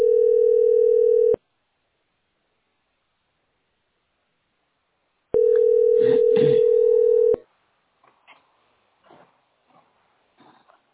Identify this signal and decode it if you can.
Ringback tone (North America), 440 + 480 Hz.
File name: 26d2e612-7648-4aec-a650